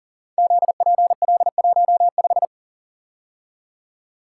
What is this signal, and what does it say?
Morse code, heard as a keyed tone, about 40 wpm, 690 Hz. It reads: ZPL15
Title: Code Groups - 5 Letter:Numbers - 40wpm